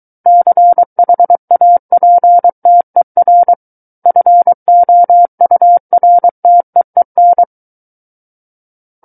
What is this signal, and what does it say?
Morse code, heard as a keyed tone, about 23 wpm, 700 Hz. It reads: CHAPTER FOURTEEN
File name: Book One / 14-Chapter 14 - 23 WPM